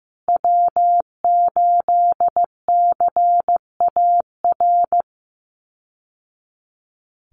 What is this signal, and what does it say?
Morse code, heard as a keyed tone, about 15 wpm, 700 Hz. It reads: W8CAR
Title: Call Signs Repeated 3x - 15wpm